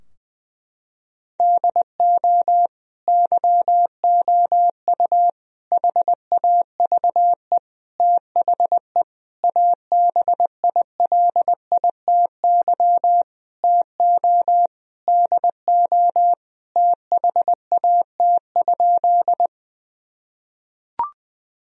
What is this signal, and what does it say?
Morse code, heard as a keyed tone, about 20 wpm, 700 Hz. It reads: DO YOU HAVE THE ABILITY TO DO THAT?
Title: Sentences from Top 1000 Words - Review Part 3 - 20wpm